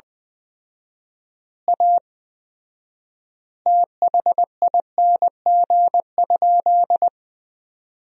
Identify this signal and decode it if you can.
Morse code, heard as a keyed tone, about 21 wpm, 700 Hz. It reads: A THING?
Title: 4x Word-Spacing - Sentences from Top 200 Words - 20wpm